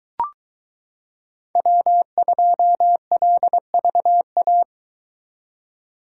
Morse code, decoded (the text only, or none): W2LVA